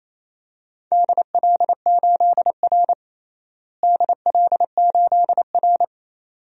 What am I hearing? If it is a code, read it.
Morse code, heard as a keyed tone, about 28 wpm, 700 Hz. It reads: DL8R DL8R
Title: DXCC-100 Call Signs - 4 Characters - Repeated 2x - 28wpm